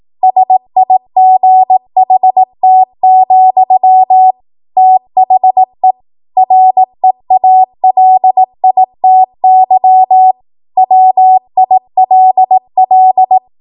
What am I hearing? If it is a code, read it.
Morse code, heard as a keyed tone, about 18 wpm, 750 Hz. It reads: SIGHT, THE REALITY WILL
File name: Morse / 18WPM